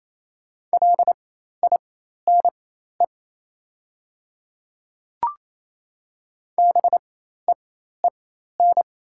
Morse code, heard as a keyed tone, about 27 wpm, spaced as for 13 wpm, 700 Hz. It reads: LINE BEEN
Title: 4x Character Spacing - Top 200 Words - 28wpm